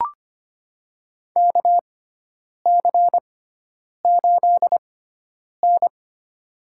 Morse code, heard as a keyed tone, about 25 wpm, spaced as for 9 wpm, 700 Hz. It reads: KC8N
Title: USA Call Signs - 6 Characters - 6x Character Spacing - 25wpm